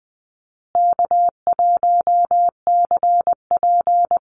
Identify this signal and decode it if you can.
Morse code, heard as a keyed tone, about 20 wpm, 700 Hz. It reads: K1CP
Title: Call Signs Repeated 2x - 20wpm